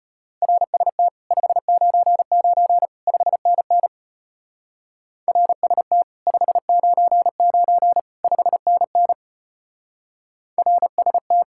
Morse code, heard as a keyed tone, about 29 wpm, 700 Hz. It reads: RST 599 5NN RST 599 5NN RST
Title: Speed-Racing Common QSO Phrases Encore - 25wpm